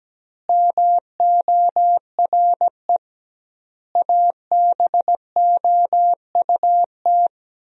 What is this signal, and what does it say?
Morse code, heard as a keyed tone, about 17 wpm, 700 Hz. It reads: MORE ABOUT
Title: 2x Word-Spacing - Sentences from Top 200 Words - 17wpm